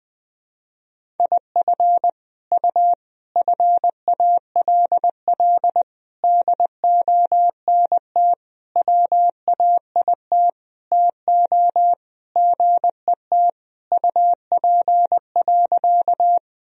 Morse code, heard as a keyed tone, about 21 wpm, 700 Hz. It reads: IF U FALL DONT WAIT TO GET UP.
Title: Sentences from Top 500 Words and Abbreviations 20wpm